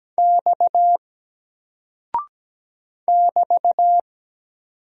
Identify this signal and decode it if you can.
Morse code, heard as a keyed tone, about 17 wpm, 700 Hz. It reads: X <BT>
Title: Single Character (Letter:Number:Punctuation:Prosign) 17wpm